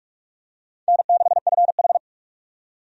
Morse code, heard as a keyed tone, about 46 wpm, 700 Hz. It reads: N6FH
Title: USA Call Signs - 4 Characters - 45wpm